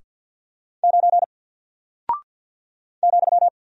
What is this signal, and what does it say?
Morse code, heard as a keyed tone, about 48 wpm, 700 Hz. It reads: 9 ,
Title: Single Character (Letter:Number:Punctuation:Prosign) Alt 50wpm